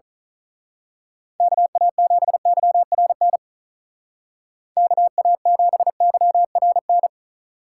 Morse code, heard as a keyed tone, about 39 wpm, 700 Hz. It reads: KA7YRN KA7YRN
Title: Speed Racing - USA 2x3 Call Signs 35wpm